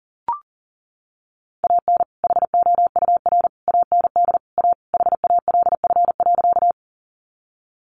Morse code, heard as a keyed tone, about 38 wpm, 700 Hz. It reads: AN HOUR AND A HALF.
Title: Sentences from Top 400 Words 40wpm